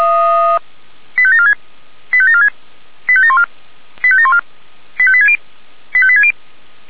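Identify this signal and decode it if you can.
Five-tone selective call (ZVEI), heard as quick runs of five tones, tones of about 80 ms. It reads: 85746 85746 85714 85714 85860 85860
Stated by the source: ZVEI five-tone call received by SDR.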